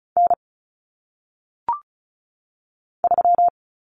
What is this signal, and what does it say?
Morse code, heard as a keyed tone, about 36 wpm, 700 Hz. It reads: N 3
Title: Common Cut Numbers - 1-Digit 35wpm